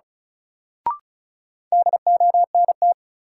Morse code, heard as a keyed tone, about 36 wpm, 700 Hz. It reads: DONT
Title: Sentences from Top 300 Words - New Word Drill - 35wpm